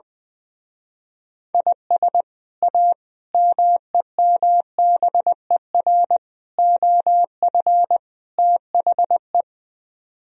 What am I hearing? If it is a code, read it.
Morse code, heard as a keyed tone, about 21 wpm, 700 Hz. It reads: IS A MEMBER OF THE
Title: Speed-Racing - Sets of 5 Words 20wpm